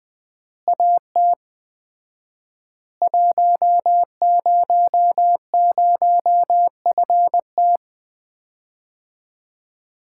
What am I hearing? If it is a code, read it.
Morse code, heard as a keyed tone, about 21 wpm, 700 Hz. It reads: AT 100FT
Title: Common QSO Phrases - 4x Word-Spacing - 20wpm